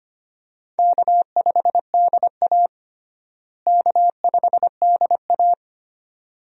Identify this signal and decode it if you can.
Morse code, heard as a keyed tone, about 25 wpm, 700 Hz. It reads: K5DA K5DA
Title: USA Call Signs - 4 Characters - Repeated 2x - 25wpm